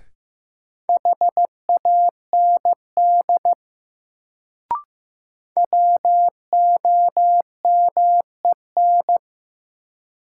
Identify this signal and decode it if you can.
Morse code, heard as a keyed tone, about 15 wpm, 700 Hz. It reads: HAND WOMEN
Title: Sets of 3 Words - New Word Drill - 15wpm